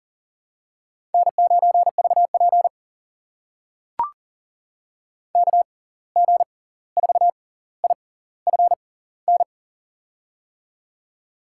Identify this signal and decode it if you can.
Morse code, heard as a keyed tone, about 40 wpm, spaced as for 18 wpm, 700 Hz. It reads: N9VP KG4IFN
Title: USA Call Signs - 6x Character Spacing - 40wpm